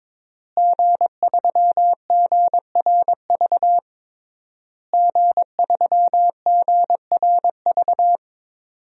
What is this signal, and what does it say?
Morse code, heard as a keyed tone, about 22 wpm, 700 Hz. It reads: G3GRV G3GRV
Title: DXCC-100 Call Signs - 5 Characters - Repeated 3x - 22wpm